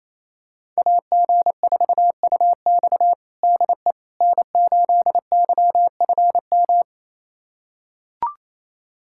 Morse code, heard as a keyed tone, about 28 wpm, 700 Hz. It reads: AG4UX DE N8YFM
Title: USA Callsign DE Callsign - 5 Characters - 28wpm